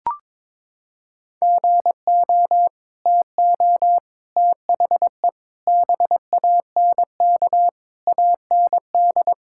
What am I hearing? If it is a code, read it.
Morse code, heard as a keyed tone, about 22 wpm, 700 Hz. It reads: GO TO THE BANK AND
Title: Sentences from Top 1000 Words - Review Part 5 - 22wpm